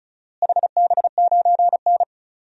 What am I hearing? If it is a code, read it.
Morse code, heard as a keyed tone, about 35 wpm, 700 Hz. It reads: HB9N
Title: DXCC-100 Call Signs - 4 Characters - Repeated 3x - 35wpm